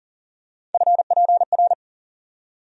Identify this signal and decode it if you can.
Morse code, heard as a keyed tone, about 40 wpm, 700 Hz. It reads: FPR
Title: USA Call Sign Postfixes - 3 Letters 40wpm